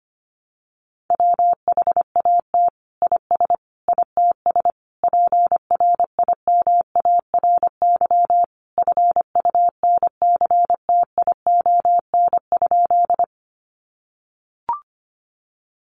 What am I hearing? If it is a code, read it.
Morse code, heard as a keyed tone, about 25 wpm, 700 Hz. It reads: WHAT IS ITS PRIMARY FUNCTION?
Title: Sentences from Top 1100 Words 25wpm